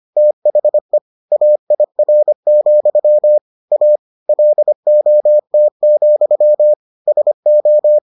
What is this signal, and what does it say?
Morse code, heard as a keyed tone, about 25 wpm, 600 Hz. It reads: THE AIR, A LOT, SO